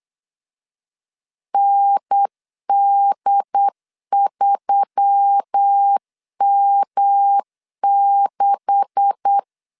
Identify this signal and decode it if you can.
Morse code, heard as a keyed tone, about 8 wpm, 800 Hz. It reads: ND3M6
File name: LEZIONI / 10 SIGNS / 10B INT 8wpm 3534